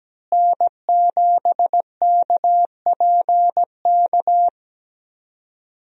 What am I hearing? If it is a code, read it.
Morse code, heard as a keyed tone, about 17 wpm, 700 Hz. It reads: N7KPK
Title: USA Call Signs - 5 Characters - Repeated 2x - 17wpm